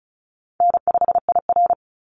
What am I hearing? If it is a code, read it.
Morse code, heard as a keyed tone, about 34 wpm, 700 Hz. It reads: N5IR
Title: Call Signs - 35wpm